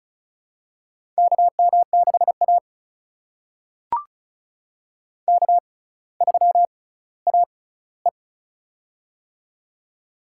Morse code, heard as a keyed tone, about 36 wpm, spaced as for 18 wpm, 700 Hz. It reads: KM6A K3AE
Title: USA Call Signs - 6x Character Spacing - 35wpm